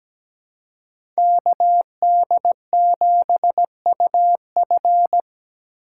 Morse code, heard as a keyed tone, about 17 wpm, 700 Hz. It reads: KD7UF
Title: USA Call Signs - 5 Characters - 4x Character Spacing - 17wpm